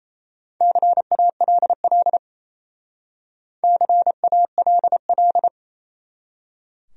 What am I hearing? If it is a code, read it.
Morse code, heard as a keyed tone, about 37 wpm, 700 Hz. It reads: CALL CALL